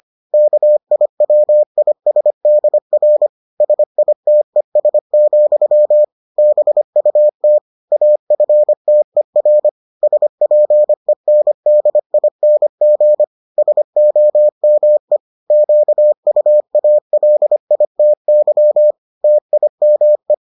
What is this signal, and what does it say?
Morse code, heard as a keyed tone, about 25 wpm, 600 Hz. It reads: KIWISDR SITES, BUT AFTER SPENDING SOME QUALITY TIME